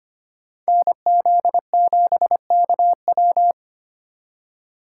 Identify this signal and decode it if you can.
Morse code, heard as a keyed tone, about 32 wpm, 700 Hz. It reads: NZ7KW